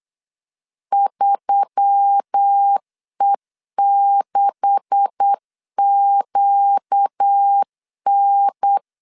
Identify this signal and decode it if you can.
Morse code, heard as a keyed tone, about 8 wpm, 800 Hz. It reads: 3E6QN
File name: CW / DET / 1 9 1500C S C 8wpm 5653